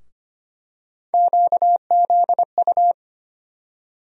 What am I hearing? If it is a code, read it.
Morse code, heard as a keyed tone, about 25 wpm, 700 Hz. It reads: QZU
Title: USA Call Sign Postfixes - 3 Letters 25wpm